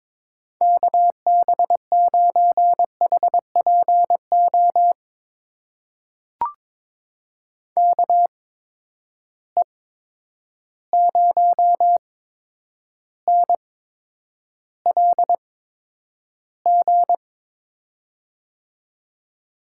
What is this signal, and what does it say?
Morse code, heard as a keyed tone, about 22 wpm, spaced as for 9 wpm, 700 Hz. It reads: KB9HPO KE0NLG
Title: USA Call Signs - 6 Characters - 8x Character Spacing - 22wpm